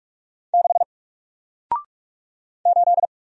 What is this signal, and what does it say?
Morse code, heard as a keyed tone, about 46 wpm, 700 Hz. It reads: 6 8
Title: Single Letter-Number Alt 45wpm